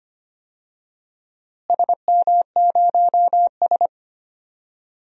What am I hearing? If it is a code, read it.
Morse code, heard as a keyed tone, about 25 wpm, 700 Hz. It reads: SM0S
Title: DXCC-100 Call Signs - 4 Characters - Repeated 2x - 25wpm